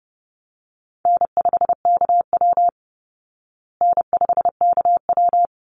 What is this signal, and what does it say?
Morse code, heard as a keyed tone, about 30 wpm, 700 Hz. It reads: N5KW N5KW
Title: Easy Call Signs Repeated 2x - 30wpm